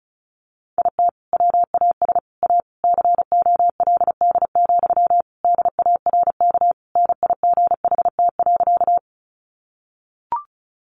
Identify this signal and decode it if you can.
Morse code, heard as a keyed tone, about 34 wpm, 700 Hz. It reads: IT WAS A COLD, DARK NIGHT.
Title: Sentences from Top 500 Words 35wpm